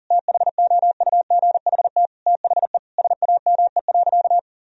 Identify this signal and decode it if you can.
Morse code, heard as a keyed tone, about 40 wpm, 700 Hz. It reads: THOUGHT THE SAME.
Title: Sentences from Top 1000 Words - Review Part 1 - 40wpm